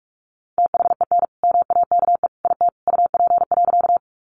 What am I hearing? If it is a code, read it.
Morse code, heard as a keyed tone, about 44 wpm, 700 Hz. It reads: THEN MAKE IT UP.
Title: Sentences from Top 100 Words 45wpm